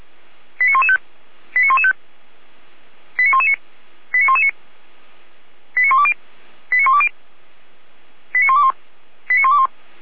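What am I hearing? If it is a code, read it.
Five-tone selective call (ZVEI), heard as quick runs of five tones, tones of about 70 ms. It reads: 89115 89115 89119 89119 89120 89120 89121 89121
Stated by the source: ZVEI five-tone call 89126 received on an SDR.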